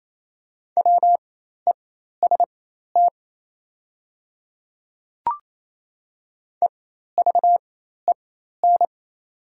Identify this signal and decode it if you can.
Morse code, heard as a keyed tone, about 28 wpm, spaced as for 13 wpm, 700 Hz. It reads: WEST EVEN
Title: 4x Character Spacing - Top 500 Words - 28wpm